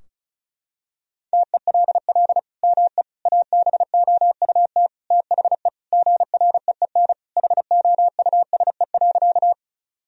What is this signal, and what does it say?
Morse code, heard as a keyed tone, about 34 wpm, 700 Hz. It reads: TELL ME ABOUT THE GREEN HOUSE.
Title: Sentences from Top 1000 Words - Review Part 2 - 35wpm